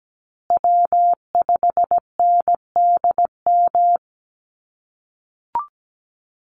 Morse code, heard as a keyed tone, about 17 wpm, 700 Hz. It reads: W5NDM
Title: USA Call Signs Repeated 2x 17wpm